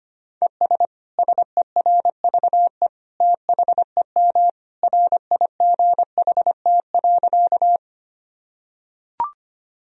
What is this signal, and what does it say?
Morse code, heard as a keyed tone, about 25 wpm, 700 Hz. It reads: ES SERVE THEM RIGHT.
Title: Sentences from Top 400 Words and Abbreviations 25wpm